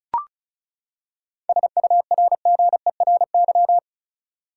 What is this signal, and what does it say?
Morse code, heard as a keyed tone, about 36 wpm, 700 Hz. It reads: SURGERY
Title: Seven Letter Words - 35wpm